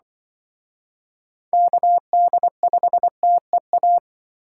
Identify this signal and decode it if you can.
Morse code, heard as a keyed tone, about 25 wpm, 700 Hz. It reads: KD5TEA
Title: Speed Racing - USA 2x3 Call Signs 20wpm